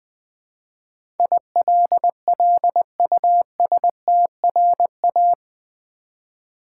Morse code, heard as a keyed tone, about 20 wpm, 700 Hz. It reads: ILLUSTRA
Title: Build A Word - 4 Letters At A Time - 12-Letter Words - 20wpm